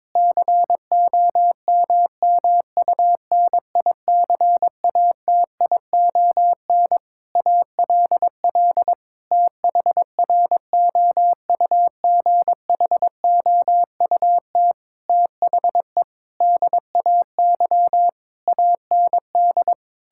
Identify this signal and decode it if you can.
Morse code, heard as a keyed tone, about 22 wpm, 700 Hz. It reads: COMMUNICATION ALL THROUGHOUT THE DAY AND
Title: Sentences from Top 1000 Words - Review Part 6 - 22wpm